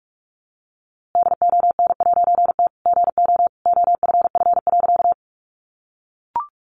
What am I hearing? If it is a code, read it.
Morse code, heard as a keyed tone, about 44 wpm, 700 Hz. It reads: DON'T GO OFF.
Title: A Sound Of Thunder 45wpm